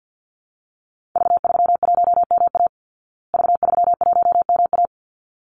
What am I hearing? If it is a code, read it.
Morse code, heard as a keyed tone, about 48 wpm, 700 Hz. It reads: V31MA V31MA
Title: Call Signs Repeated 3x - 50wpm